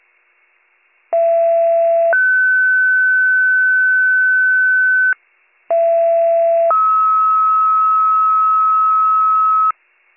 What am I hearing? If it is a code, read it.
Two-tone page, heard as a one-second tone, then a three-second tone, about 670 then 1500 Hz, about 670 then 1290 Hz. It is heard from 1.1 to 9.7 s.